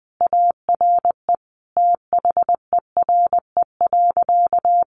Morse code, heard as a keyed tone, about 20 wpm, 700 Hz. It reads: ARE THERE.
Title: Sentences from Top 500 Words 20wpm